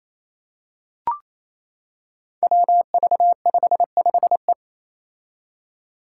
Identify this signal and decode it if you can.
Morse code, heard as a keyed tone, about 28 wpm, 700 Hz. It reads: WV55E
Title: Five Challenging Letter-Number Combinations 28wpm